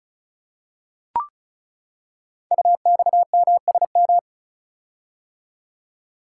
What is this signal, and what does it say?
Morse code, heard as a keyed tone, about 36 wpm, 700 Hz. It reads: UXMSM
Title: Code Groups - 5 Letters - 35wpm